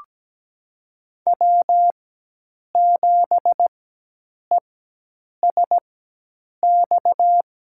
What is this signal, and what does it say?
Morse code, heard as a keyed tone, about 17 wpm, spaced as for 8 wpm, 700 Hz. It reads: W7ESX
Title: USA Call Signs - 5 Characters - 4x Character Spacing - 17wpm